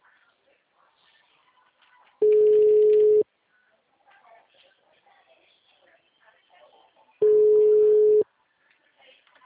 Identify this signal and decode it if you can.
Ringback tone (Europe, 425 Hz), with speech.